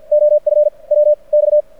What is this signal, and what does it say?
Morse code, heard as a keyed tone, about 49 wpm, spaced as for 36 wpm, 600 Hz. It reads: OF WX
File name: archived / CRSnet